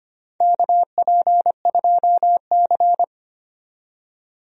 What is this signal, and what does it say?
Morse code, heard as a keyed tone, about 25 wpm, 700 Hz. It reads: KP2C
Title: DXCC-100 Call Signs - 4 Characters - 25wpm